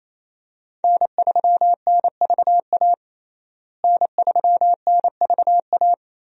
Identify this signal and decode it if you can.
Morse code, heard as a keyed tone, about 28 wpm, 700 Hz. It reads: N3NVA N3NVA
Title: USA Call Signs - 5 Characters - Repeated 3x - 28wpm